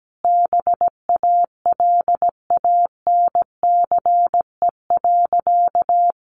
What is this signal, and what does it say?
Morse code, heard as a keyed tone, about 17 wpm, 700 Hz. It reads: BALANCE.
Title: Sentences from Top 1200 Words 17wpm